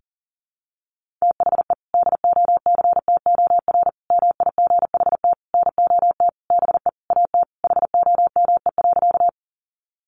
Morse code, heard as a keyed tone, about 40 wpm, 700 Hz. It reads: THE DOCTOR MIGHT NOT BE AT HOME.
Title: Sentences from Top 1500 Words 40wpm